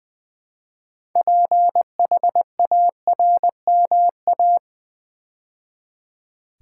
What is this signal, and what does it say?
Morse code, heard as a keyed tone, about 20 wpm, 700 Hz. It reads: PHARMA
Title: Build A Word - 3 Letters At A Time - 8-Letter Words - 20wpm